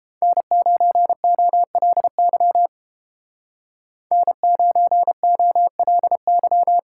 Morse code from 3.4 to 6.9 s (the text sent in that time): N9OLY